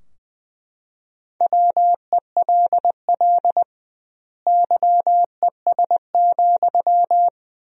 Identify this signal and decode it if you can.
Morse code, heard as a keyed tone, about 20 wpm, 700 Hz. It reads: WELL YES,
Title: 2x Word-Spacing - Sentences from Top 500 Words - 20wpm